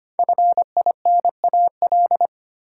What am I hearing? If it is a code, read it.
Morse code, heard as a keyed tone, about 25 wpm, 700 Hz. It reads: FINAL
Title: Sentences from Top 800 Words - New Word Drill - 25wpm